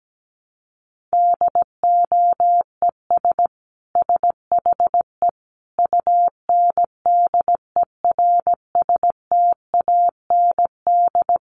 Morse code, heard as a keyed tone, about 17 wpm, 700 Hz. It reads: DOES SHE UNDERSTAND
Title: Sentences from Top 1100 Words 17wpm